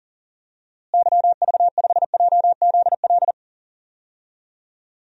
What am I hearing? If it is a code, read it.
Morse code, heard as a keyed tone, about 40 wpm, 700 Hz. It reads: YV5JZL
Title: DXCC-100 Call Signs - 6 Characters - Repeated 2x - 40wpm